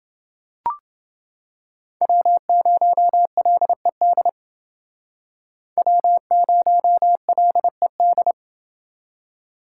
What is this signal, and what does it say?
Morse code, heard as a keyed tone, about 29 wpm, 700 Hz. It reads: W0LED W0LED